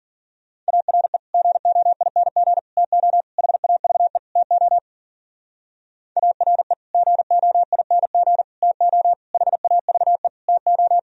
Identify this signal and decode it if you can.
Morse code, heard as a keyed tone, about 51 wpm, 700 Hz. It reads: ARE GOING TO HAVE TO ARE GOING TO HAVE TO